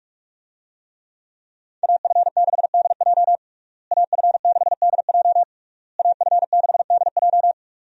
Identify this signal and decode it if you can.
Morse code, heard as a keyed tone, about 46 wpm, 700 Hz. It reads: AF6DJ AF6DJ AF6DJ
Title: USA Call Signs - 5 Characters - Repeated 3x - 45wpm